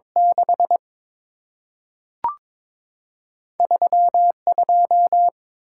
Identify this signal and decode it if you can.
Morse code, heard as a keyed tone, about 22 wpm, 690 Hz. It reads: 6 3 2
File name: practice / 22wpm / Two Letter-Number Combinations - 22wpm